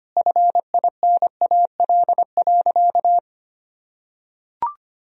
Morse code, heard as a keyed tone, about 25 wpm, 700 Hz. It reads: FINAL.
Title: Sentences from Top 1000 Words - Review Part 2 - 25wpm